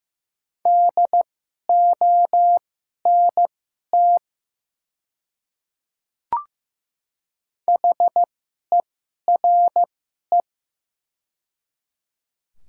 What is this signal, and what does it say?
Morse code, heard as a keyed tone, about 15 wpm, spaced as for 11 wpm, 700 Hz. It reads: DONT HERE
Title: 2x Character Spacing - Top 500 Words - 15wpm